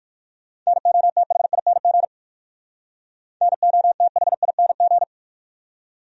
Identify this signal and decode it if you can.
Morse code, heard as a keyed tone, about 58 wpm, 690 Hz. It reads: NOTHING NOTHING